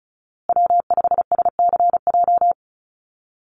Morse code, heard as a keyed tone, about 34 wpm, 700 Hz. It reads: W5SCJ
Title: USA Call Signs Repeated 2x 35wpm